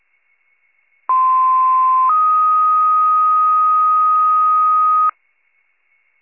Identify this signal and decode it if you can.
Two-tone page, heard as a one-second tone, then a three-second tone, about 1040 then 1250 Hz. It is heard from 1.1 to 5.1 s.